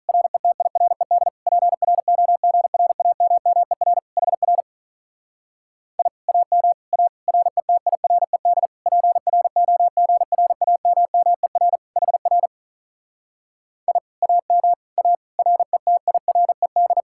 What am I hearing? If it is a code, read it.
Morse code, heard as a keyed tone, about 45 wpm, 700 Hz. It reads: RETIRED PROGRAMMER HR I AM A RETIRED PROGRAMMER HR I AM A RETIRED